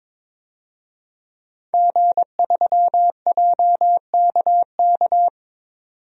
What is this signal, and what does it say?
Morse code, heard as a keyed tone, about 22 wpm, 700 Hz. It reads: G3JKK
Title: DXCC-100 Call Signs - 5 Characters - 22wpm